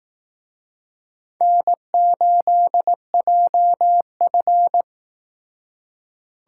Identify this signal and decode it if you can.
Morse code, heard as a keyed tone, about 18 wpm, 700 Hz. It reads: N8JF
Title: Speed Racing - USA Call Signs 15wpm